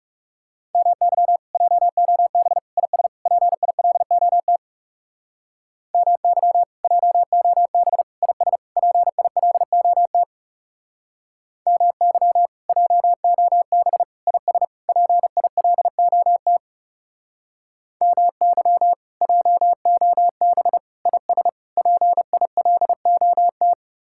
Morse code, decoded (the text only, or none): MY JOB IS PILOT MY JOB IS PILOT MY JOB IS PILOT MY JOB IS PILOT